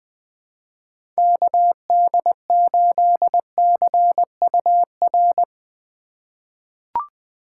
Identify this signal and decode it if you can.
Morse code, heard as a keyed tone, about 20 wpm, 700 Hz. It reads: KD8CUR
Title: USA Call Signs - 6 Characters - 3x Character Spacing - 20wpm